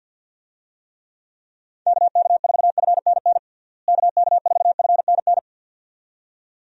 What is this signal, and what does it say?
Morse code, heard as a keyed tone, about 53 wpm, 700 Hz. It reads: KK4FNN KK4FNN
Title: USA Call Signs - 6 Characters - Repeated 2x - 50wpm